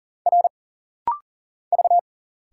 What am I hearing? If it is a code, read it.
Morse code, heard as a keyed tone, about 40 wpm, 700 Hz. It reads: R V
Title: Single Letters - Rapid-Fire 40wpm